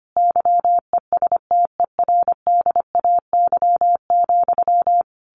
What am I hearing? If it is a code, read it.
Morse code, heard as a keyed tone, about 25 wpm, 700 Hz. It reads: YESTERDAY,
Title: A Sound Of Thunder 25wpm